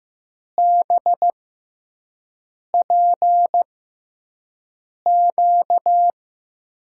Morse code, heard as a keyed tone, about 15 wpm, spaced as for 5 wpm, 700 Hz. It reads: BPQ
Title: USA Call Signs - 5 Characters - 6x Character Spacing - 15wpm